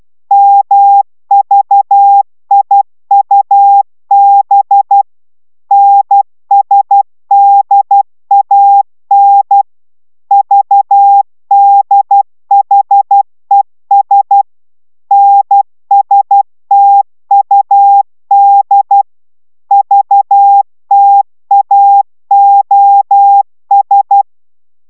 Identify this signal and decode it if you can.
Morse code, heard as a keyed tone, about 12 wpm, 800 Hz. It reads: MVIUB NSDAN VDHES NSTUD VTAOS